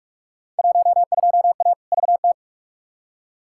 Morse code, heard as a keyed tone, about 47 wpm, 700 Hz. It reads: 12A VT
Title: Field Day Exchanges Encore - 45wpm